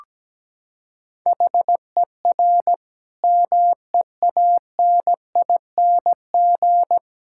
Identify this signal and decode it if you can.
Morse code, heard as a keyed tone, about 17 wpm, 700 Hz. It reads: HER MEANING
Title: Sentences from Top 1000 Words - Review Part 6 - 17wpm